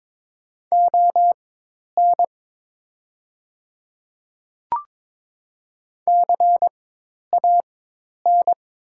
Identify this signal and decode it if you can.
Morse code, heard as a keyed tone, about 22 wpm, spaced as for 10 wpm, 700 Hz. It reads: ON CAN
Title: 4x Character Spacing - Top 100 Words - 22wpm